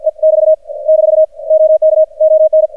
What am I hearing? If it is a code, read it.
Morse code, heard as a keyed tone, about 46 wpm, 630 Hz. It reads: E <BT> - OK ON
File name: archived / CRSnet